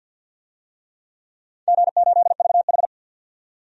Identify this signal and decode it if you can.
Morse code, heard as a keyed tone, about 52 wpm, 700 Hz. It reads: G8VH
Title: DXCC-100 Call Signs - 4 Characters - 50wpm